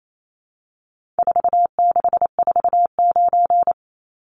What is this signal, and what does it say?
Morse code, heard as a keyed tone, about 28 wpm, 700 Hz. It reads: 4649
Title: Numbers - 4-Digit 28wpm